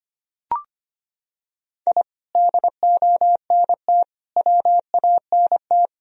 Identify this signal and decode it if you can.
Morse code, heard as a keyed tone, about 25 wpm, 700 Hz. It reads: I DONT WANT
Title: Sentences from Top 1000 Words - Review Part 1 - 25wpm